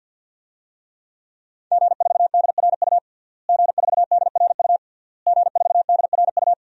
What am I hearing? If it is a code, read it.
Morse code, heard as a keyed tone, about 52 wpm, 700 Hz. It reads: G4DRU G4DRU G4DRU
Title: DXCC-100 Call Signs - 5 Characters - Repeated 3x - 50wpm